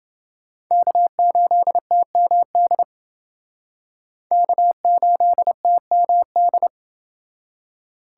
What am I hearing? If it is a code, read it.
Morse code, heard as a keyed tone, about 29 wpm, 700 Hz. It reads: K8TMD K8TMD